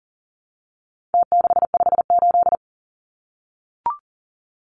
Morse code, heard as a keyed tone, about 37 wpm, 700 Hz. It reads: T658
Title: Numbers - Contest Style - 4-Digit 40wpm